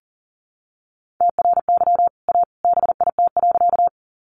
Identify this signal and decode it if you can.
Morse code, heard as a keyed tone, about 40 wpm, 700 Hz. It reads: TRY A BIT.
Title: Sentences from Top 900 Words 40wpm